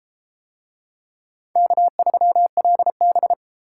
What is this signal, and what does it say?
Morse code, heard as a keyed tone, about 32 wpm, 700 Hz. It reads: K3LB